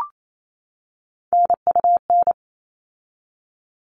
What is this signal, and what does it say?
Morse code, heard as a keyed tone, about 28 wpm, 700 Hz. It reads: NUN
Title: All Cut Numbers - 3-Digit 28wpm